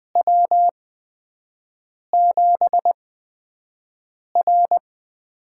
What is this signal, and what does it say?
Morse code, heard as a keyed tone, about 20 wpm, spaced as for 5 wpm, 700 Hz. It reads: W7R
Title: USA Call Signs - 8x Character Spacing - 20wpm